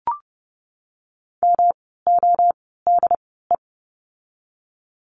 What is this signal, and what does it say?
Morse code, heard as a keyed tone, about 30 wpm, spaced as for 17 wpm, 700 Hz. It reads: MODE
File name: course / 30:10wpm / 108 Lesson 18f - Letter m - Words 30wpm